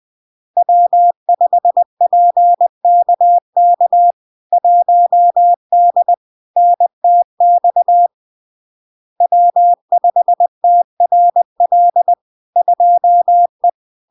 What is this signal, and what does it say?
Morse code, heard as a keyed tone, about 20 wpm, 700 Hz. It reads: W5PKK 1D NTX W5TRL 2E
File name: LARK-FD-2022-Log-in-CW-20wpm-700-Hz